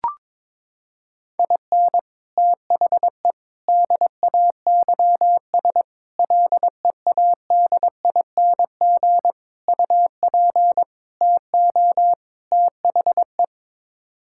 Morse code, heard as a keyed tone, about 22 wpm, 700 Hz. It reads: IN THE DAYS LEADING UP TO THE
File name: practice / 22wpm / Sets of 7 Words - 22wpm